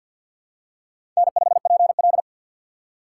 Morse code, heard as a keyed tone, about 49 wpm, 700 Hz. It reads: N5PL
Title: USA Call Signs - 6x Character Spacing - 50wpm